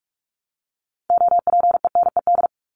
Morse code, heard as a keyed tone, about 44 wpm, 700 Hz. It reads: OPENED
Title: ED Suffix - Top 100 Words with Base Words 45wpm